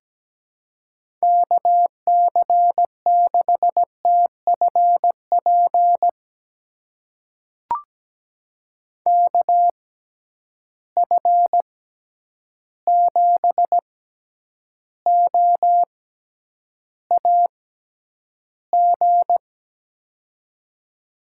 Morse code, heard as a keyed tone, about 17 wpm, spaced as for 9 wpm, 700 Hz. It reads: KC6TFP KF7OAG
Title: USA Call Signs - 6 Characters - 6x Character Spacing - 17wpm